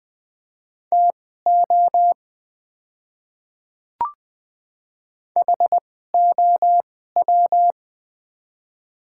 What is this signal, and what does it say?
Morse code, heard as a keyed tone, about 20 wpm, spaced as for 14 wpm, 700 Hz. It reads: TO HOW
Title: 2x Character Spacing - Top 100 Words - 20wpm